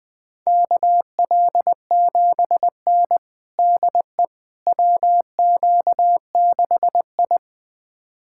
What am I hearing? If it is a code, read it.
Morse code, heard as a keyed tone, about 20 wpm, 700 Hz. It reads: KL7N DE WQ6I
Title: USA Callsign DE Callsign - 4 Characters - 20wpm